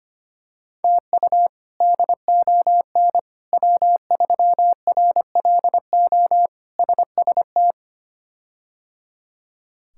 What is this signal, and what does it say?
Morse code, heard as a keyed tone, about 25 wpm, 700 Hz. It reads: TU DON W3RLO SST
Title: K1USN SST QSO Practice - 25wpm